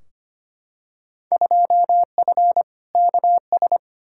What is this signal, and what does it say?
Morse code, heard as a keyed tone, about 25 wpm, 700 Hz. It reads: 2F KS
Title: Field Day Exchanges - 25wpm